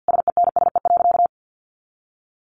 Morse code, heard as a keyed tone, about 49 wpm, 700 Hz. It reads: SENSE.
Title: Sentences from Top 900 Words 50wpm